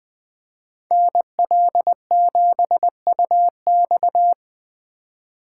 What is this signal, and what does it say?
Morse code, heard as a keyed tone, about 20 wpm, 700 Hz. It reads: NL7UX
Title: Speed Racing - USA 2x2 Call Signs 17wpm